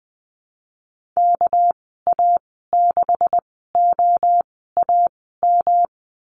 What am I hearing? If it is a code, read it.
Morse code, heard as a keyed tone, about 20 wpm, spaced as for 14 wpm, 700 Hz. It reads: KA6OAM
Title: 219 Lesson 35g - Number 0 - Call Signs 20wpm